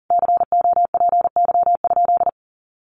Morse code, heard as a keyed tone, about 40 wpm, 700 Hz. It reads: COPY?
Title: Sentences from Top 1400 Words 40wpm